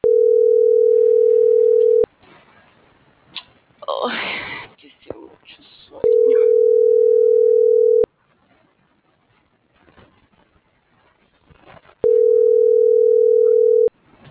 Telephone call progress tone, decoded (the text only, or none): ringback tone